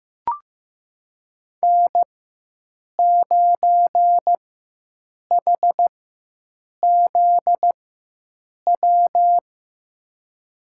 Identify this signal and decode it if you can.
Morse code, heard as a keyed tone, about 15 wpm, spaced as for 7 wpm, 700 Hz. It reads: N9HZW
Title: USA Call Signs - 5 Characters - 4x Character Spacing - 15wpm